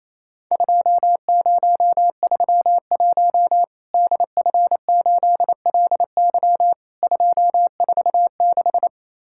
Morse code, heard as a keyed tone, about 28 wpm, 700 Hz. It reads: 2031 DF8LY 246
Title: Worked All Europe DX Contest - QTC Exchange - 28wpm